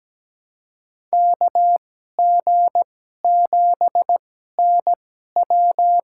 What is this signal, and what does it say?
Morse code, heard as a keyed tone, about 17 wpm, spaced as for 12 wpm, 700 Hz. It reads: KG7NW
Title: USA Call Signs - 2x Character Spacing - 17wpm